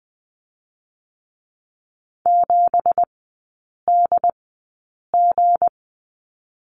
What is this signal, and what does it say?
Morse code, heard as a keyed tone, about 20 wpm, spaced as for 8 wpm, 700 Hz. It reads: 7DG
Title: 198 Lesson 32e - Letter k - Three Characters 20wpm